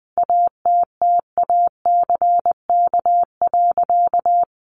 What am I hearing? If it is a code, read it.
Morse code, heard as a keyed tone, about 20 wpm, 700 Hz. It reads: ATTACK.
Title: Sentences from Top 1500 Words 20wpm